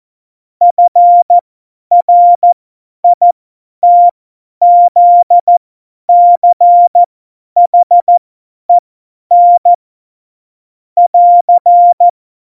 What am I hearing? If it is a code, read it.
Morse code, heard as a keyed tone, about 14 wpm, spaced as for 10 wpm, 700 Hz. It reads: FRITZCHEN <AR>
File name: Morsekurs / Witze / Witz 19. September 17, Tempo 50